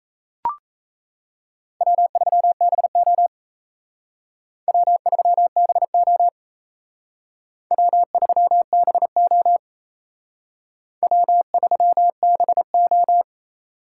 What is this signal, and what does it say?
Morse code, heard as a keyed tone, about 35 wpm, 700 Hz. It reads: W3BO W3BO W3BO W3BO
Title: Speed Racing - USA Call Signs 28wpm